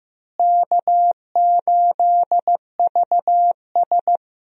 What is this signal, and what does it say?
Morse code, heard as a keyed tone, about 15 wpm, 700 Hz. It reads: K8VS
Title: USA Call Signs - 4 Characters - Repeated 3x - 15wpm